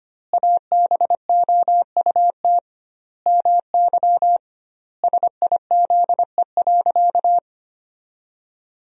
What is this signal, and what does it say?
Morse code, heard as a keyed tone, about 25 wpm, 700 Hz. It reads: ABOUT MY SIZE.
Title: 2x Word-Spacing - Sentences from Top 400 Words - 25wpm